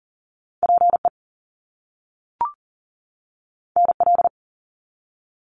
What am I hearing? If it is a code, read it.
Morse code, heard as a keyed tone, about 38 wpm, 700 Hz. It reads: PE NL
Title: Canadian Provinces and Territories 40wpm